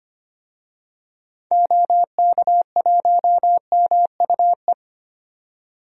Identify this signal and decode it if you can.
Morse code, heard as a keyed tone, about 25 wpm, 700 Hz. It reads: OK1MUE
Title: DXCC-100 Call Signs - 6 Characters - 25wpm